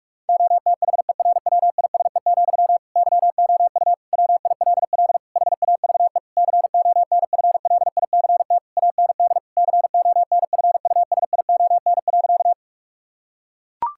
Morse code, heard as a keyed tone, about 46 wpm, 700 Hz. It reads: OTHERWISE, YOU WILL HAVE CONFLICT AND CONFUSION.
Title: Sentences from Top 1600 Words 45wpm